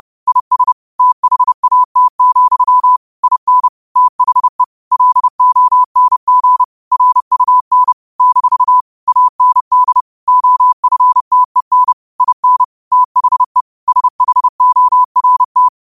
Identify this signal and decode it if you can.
Morse code, heard as a keyed tone, about 30 wpm, 1000 Hz. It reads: IS THAT, IN THE LONG RUN - AND OFTEN IN THE SHORT
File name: 30wpm